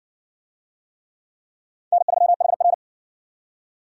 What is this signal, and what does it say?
Morse code, heard as a keyed tone, about 58 wpm, 700 Hz. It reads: N3HR